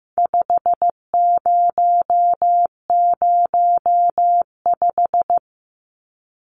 Morse code, heard as a keyed tone, about 15 wpm, 700 Hz. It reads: 5005
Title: Numbers - 4-Digit 15wpm